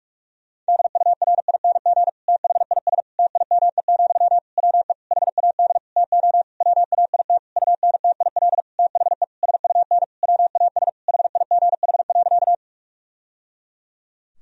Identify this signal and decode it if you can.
Morse code, heard as a keyed tone, about 41 wpm, 700 Hz. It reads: DURING THIS TIME, WE HAD TO WAIT UNTIL THE SUN WAS HIGH.
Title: Sentences from Top 1000 Words - Review Part 2 - 45wpm